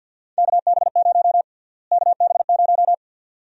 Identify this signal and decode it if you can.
Morse code, heard as a keyed tone, about 49 wpm, 700 Hz. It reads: KB0 KB0
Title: USA Call Sign Prefixes - Repeated 2x - 50wpm